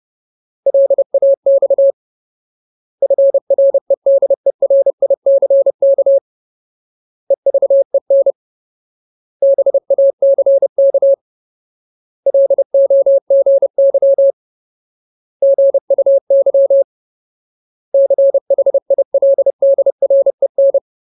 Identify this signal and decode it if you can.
Morse code, heard as a keyed tone, about 30 wpm, 550 Hz. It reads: LAX FREDERICK EVEN BACK LOGY GUY CHILDREN
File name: Ultimate Word List / 30WPM Ultimate